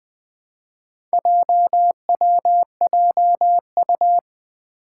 Morse code, heard as a keyed tone, about 20 wpm, 700 Hz. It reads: JWJU
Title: Four Challenging Letter-Number Combinations 20wpm